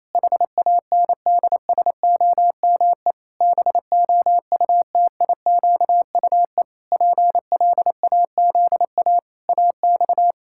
Morse code, heard as a keyed tone, about 28 wpm, 700 Hz. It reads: HANDSOME BOUTIQUE PLAZA AX
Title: Pangram Challenge v2 - 28wpm